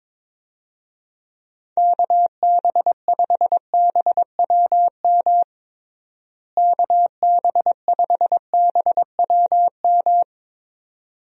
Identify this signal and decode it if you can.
Morse code, heard as a keyed tone, about 22 wpm, 700 Hz. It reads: KB5BWM KB5BWM
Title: USA Call Signs - 6 Characters - Repeated 2x - 22wpm